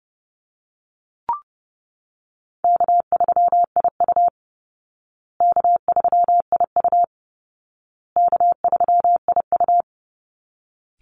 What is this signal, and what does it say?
Morse code, heard as a keyed tone, about 30 wpm, 700 Hz. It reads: K3IU K3IU K3IU
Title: Call Signs Repeated 2x - 30wpm